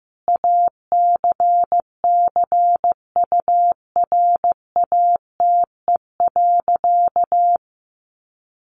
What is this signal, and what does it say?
Morse code, heard as a keyed tone, about 15 wpm, 700 Hz. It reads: ACCURATE.
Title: Sentences from Top 1200 Words 15wpm